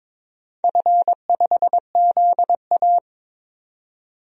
Morse code, heard as a keyed tone, about 22 wpm, 700 Hz. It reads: F5ZA
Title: DXCC-100 Call Signs - 4 Characters - 22wpm